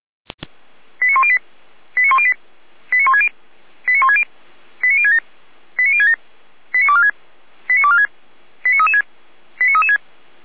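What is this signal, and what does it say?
Five-tone selective call (ZVEI), heard as quick runs of five tones, tones of about 70 ms. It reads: 89108 89108 89160 89160 89067 89067 89246 89246 89336 89336
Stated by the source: ZVEI five-tone selective call received by SDR.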